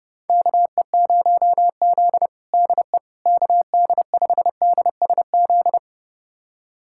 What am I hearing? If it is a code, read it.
Morse code, heard as a keyed tone, about 30 wpm, 700 Hz. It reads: KE0Z DE KD5DSZ
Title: USA Callsign DE Callsign 30wpm